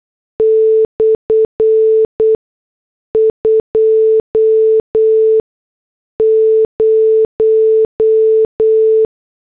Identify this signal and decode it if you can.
Morse code, heard as a keyed tone, about 8 wpm, spaced as for 6 wpm, 440 Hz. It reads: /20